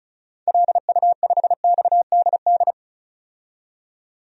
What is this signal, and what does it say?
Morse code, heard as a keyed tone, about 35 wpm, 700 Hz. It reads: LU5XDD
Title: DXCC-100 Call Signs - 6 Characters - 35wpm